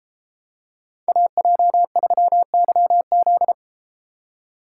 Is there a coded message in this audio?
Yes: Morse code from 1.1 to 3.5 s, heard as a keyed tone, about 33 wpm, 700 Hz. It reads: AJ3YZ